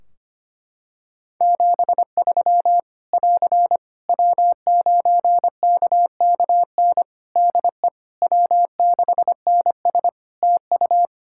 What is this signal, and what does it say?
Morse code, heard as a keyed tone, about 25 wpm, 700 Hz. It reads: 73 <AR> W9KKN DE W6NS TU
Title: Practice QSO - Exchange 4 - Ending - 25wpm